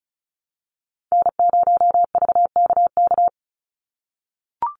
Morse code, heard as a keyed tone, about 36 wpm, 700 Hz. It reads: N0VKK
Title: USA Call Signs 35wpm